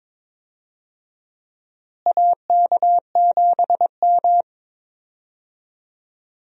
Morse code, heard as a keyed tone, about 22 wpm, 700 Hz. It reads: AK7M
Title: USA Call Signs - 4 Characters - Repeated 2x - 22wpm